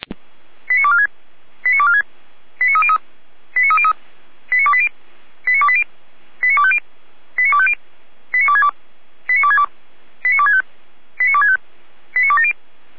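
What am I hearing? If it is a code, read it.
Five-tone selective call (ZVEI), heard as quick runs of five tones, tones of about 70 ms. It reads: 89246 89246 89303 89303 89280 89280 89250 89250 89262 89262 89265 89265 89270
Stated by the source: ZVEI five-tone selective call received on an SDR.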